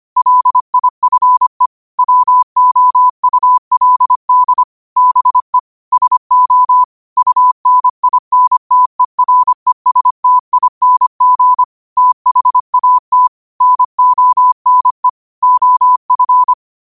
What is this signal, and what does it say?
Morse code, heard as a keyed tone, about 25 wpm, 1000 Hz. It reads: LIFE WOULD BE SO UNINTERESTING THAT NONE OF